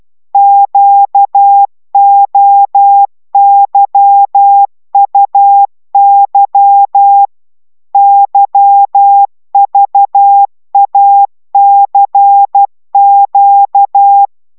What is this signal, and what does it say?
Morse code, heard as a keyed tone, about 12 wpm, 800 Hz. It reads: QOYUY YVACQ